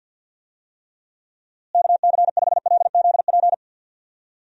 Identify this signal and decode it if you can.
Morse code, heard as a keyed tone, about 48 wpm, 700 Hz. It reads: KC5LZP
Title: USA Call Signs - 6 Characters - Repeated 2x - 50wpm